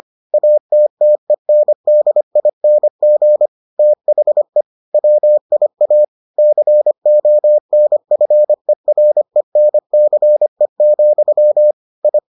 Morse code, heard as a keyed tone, about 25 wpm, 600 Hz. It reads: ATTENDING THE WIA CONFERENCE, I